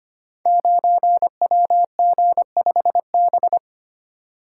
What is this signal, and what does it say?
Morse code, heard as a keyed tone, about 25 wpm, 700 Hz. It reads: 9WG5B
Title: Code Groups - 5 Letter:Numbers - 25wpm